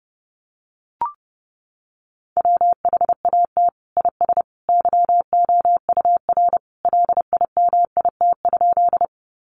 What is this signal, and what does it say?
Morse code, heard as a keyed tone, about 30 wpm, 700 Hz. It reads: WHAT IS YOUR LIMIT?
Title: Sentences from Top 1500 Words 30wpm